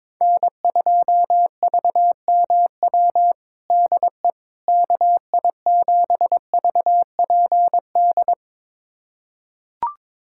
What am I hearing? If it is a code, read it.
Morse code, heard as a keyed tone, about 22 wpm, 700 Hz. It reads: N2VMW DE KI7VPD
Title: USA Callsign DE Callsign 22wpm